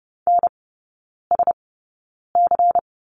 Morse code, heard as a keyed tone, about 28 wpm, spaced as for 9 wpm, 690 Hz. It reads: NSC
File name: Lesson 16e - Letter c - Three Characters 30wpm